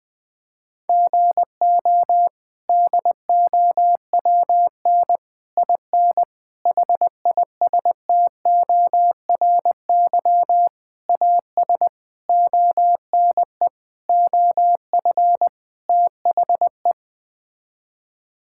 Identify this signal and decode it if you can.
Morse code, heard as a keyed tone, about 20 wpm, 700 Hz. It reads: GO DOWN IN HISTORY AS ONE OF THE
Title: Sets of 8 Words - 20wpm